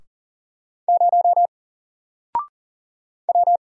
Morse code, heard as a keyed tone, about 40 wpm, 700 Hz. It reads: 0 W
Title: Single Character (Letter:Number:Punctuation:Prosign) Alt 40wpm